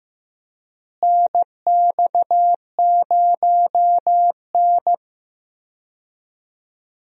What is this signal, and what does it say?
Morse code, heard as a keyed tone, about 15 wpm, 700 Hz. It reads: NX0N
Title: USA Call Signs - 4 Characters - Repeated 2x - 15wpm